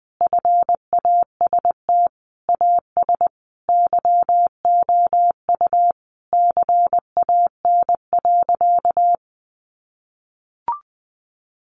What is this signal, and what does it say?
Morse code, heard as a keyed tone, about 20 wpm, 700 Hz. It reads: FAST AS YOU CAN.
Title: Sentences from Top 600 Words 20wpm